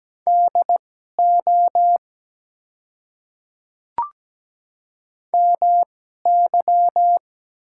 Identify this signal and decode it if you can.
Morse code, heard as a keyed tone, about 17 wpm, spaced as for 12 wpm, 700 Hz. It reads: DO MY
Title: 2x Character Spacing - Top 100 Words - 17wpm